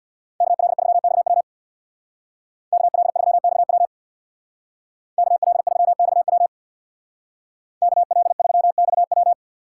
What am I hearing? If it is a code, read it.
Morse code, heard as a keyed tone, about 60 wpm, 700 Hz. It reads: KL3XW KL3XW KL3XW KL3XW
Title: Speed Racing - USA Call Signs 50wpm